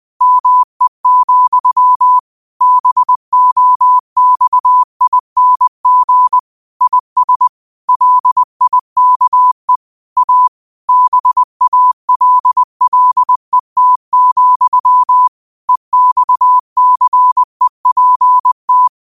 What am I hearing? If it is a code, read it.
Morse code, heard as a keyed tone, about 20 wpm, 1000 Hz. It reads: ME, BOXING IS LIKE A BALLET, EXCEPT
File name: 20wpm